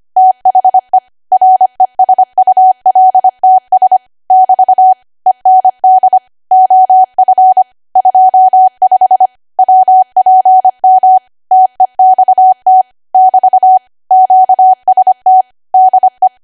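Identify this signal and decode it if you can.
Morse code, heard as a keyed tone, about 25 wpm, 750 Hz. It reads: THE RESULTS <BT> END OF 25 WPM TEXT <BT> QST DE
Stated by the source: ARRL 25 WPM code practice archive.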